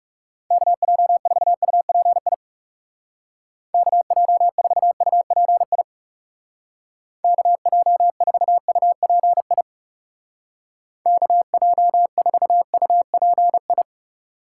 Morse code, heard as a keyed tone, about 35 wpm, 700 Hz. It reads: KJ4UPI KJ4UPI KJ4UPI KJ4UPI
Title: Speed Racing - USA 2x3 Call Signs 30wpm